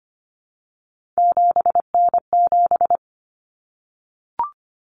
Morse code, heard as a keyed tone, about 25 wpm, 700 Hz. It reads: 7N7
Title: Common Cut Numbers - 3-Digit 25wpm